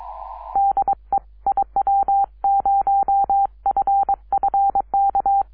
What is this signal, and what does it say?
Morse code, heard as a keyed tone, about 22 wpm, 790 Hz. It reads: DE IW0FFK
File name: iw0ffk / 2006 06 10 w8gg iw0ffk